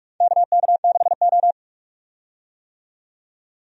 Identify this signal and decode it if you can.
Morse code, heard as a keyed tone, about 46 wpm, 700 Hz. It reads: KK6O
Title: USA Call Signs - 4 Characters - 45wpm